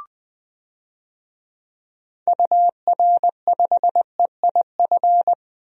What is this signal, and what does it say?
Morse code, heard as a keyed tone, about 20 wpm, 700 Hz. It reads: UR5EIF
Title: DXCC-100 Call Signs - 6 Characters - 20wpm